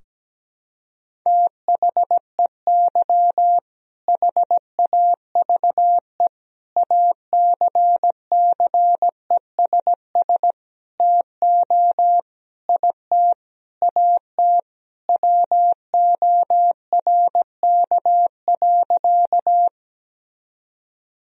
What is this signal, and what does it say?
Morse code, heard as a keyed tone, about 17 wpm, 700 Hz. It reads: THEY HAVE ACCESS TO IT AT WORK.
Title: Sentences from Top 1000 Words - Review Part 4 - 17wpm